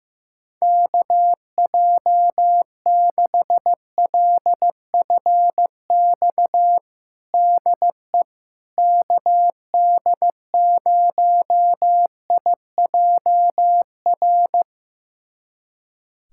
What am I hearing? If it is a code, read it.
Morse code, heard as a keyed tone, about 16 wpm, 700 Hz. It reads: KJ6LFX DE KD0IJR
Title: USA Callsign DE Callsign - 6 Characters - 15wpm